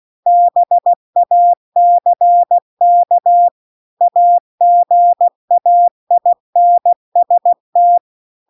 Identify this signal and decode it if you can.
Morse code, heard as a keyed tone, about 16 wpm, 700 Hz. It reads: BACK AGAINST
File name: CW-The-Big-Twoheaded-River-2-16WPM